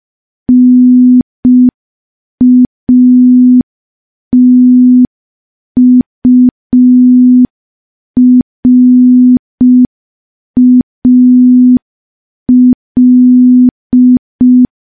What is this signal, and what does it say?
Morse code, heard as a keyed tone, about 5 wpm, 250 Hz. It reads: NATURAL
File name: morse / morse-training-05WPM